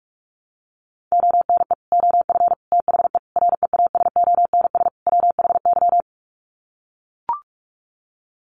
Morse code, heard as a keyed tone, about 44 wpm, 700 Hz. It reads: ONE OF THE REASONS WHY
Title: Sets of 5 Words 45wpm